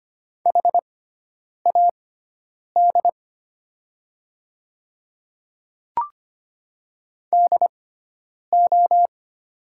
Morse code, heard as a keyed tone, about 25 wpm, spaced as for 9 wpm, 690 Hz. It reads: HAD DO
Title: 6x Character Spacing - Top 100 Words - 25wpm